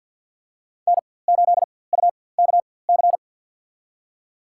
Morse code, heard as a keyed tone, about 52 wpm, spaced as for 23 wpm, 700 Hz. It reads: N8UKC
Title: USA Call Signs - 4x Character Spacing - 50wpm